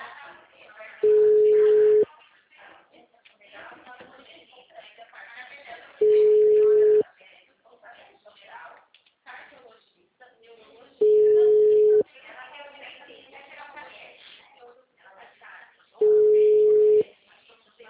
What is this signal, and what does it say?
Ringback tone (Europe, 425 Hz), with speech.
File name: out- -2009- - - .11869